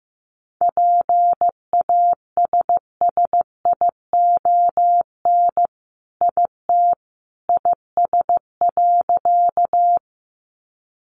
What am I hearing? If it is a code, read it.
Morse code, heard as a keyed tone, about 15 wpm, 700 Hz. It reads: PASSION IT IS.
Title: Sentences from Top 1300 Words 15wpm